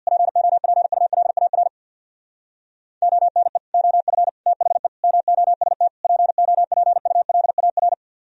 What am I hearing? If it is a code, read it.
Morse code, heard as a keyed tone, about 65 wpm, 700 Hz. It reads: POPULAR ONE OF THE MOST POPULAR